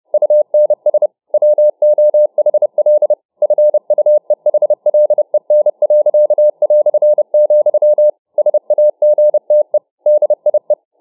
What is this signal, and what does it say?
Morse code, heard as a keyed tone, about 29 wpm, 600 Hz. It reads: UNS WOHL FUEHLEN.", SAGTE DIE